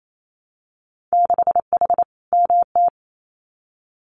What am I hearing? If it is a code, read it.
Morse code, heard as a keyed tone, about 28 wpm, 700 Hz. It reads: 6H MT
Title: Winter Field Day Exchanges Encore 28wpm